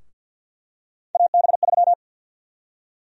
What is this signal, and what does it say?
Morse code, heard as a keyed tone, about 52 wpm, 700 Hz. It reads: AB3
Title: USA Call Sign Prefixes - 50wpm